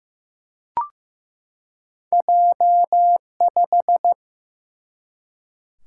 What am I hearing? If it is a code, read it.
Morse code, heard as a keyed tone, about 15 wpm, 700 Hz. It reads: J5
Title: Single-Letter Single-Number 15wpm